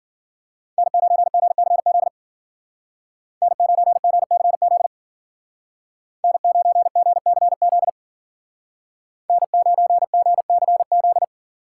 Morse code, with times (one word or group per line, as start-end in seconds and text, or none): 0.8-2.1 N9GCZ
3.4-4.9 N9GCZ
6.2-7.9 N9GCZ
9.3-11.2 N9GCZ